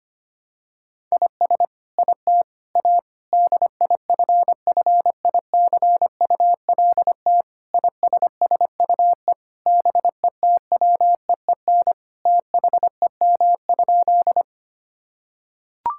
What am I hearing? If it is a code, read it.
Morse code, heard as a keyed tone, about 25 wpm, 700 Hz. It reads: IS IT A DIFFICULT ISSUE BETWEEN THEM?
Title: Sentences from Top 1000 Words - Review Part 6 - 25wpm